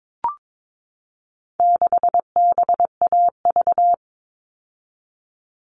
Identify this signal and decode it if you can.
Morse code, heard as a keyed tone, about 22 wpm, 700 Hz. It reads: 6BAV
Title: All Cut Numbers - 4-Digit 22wpm